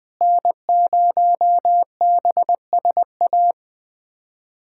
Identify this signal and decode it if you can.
Morse code, heard as a keyed tone, about 20 wpm, 700 Hz. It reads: N0BSA
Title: USA Callsign DE Callsign - 5 Characters - 20wpm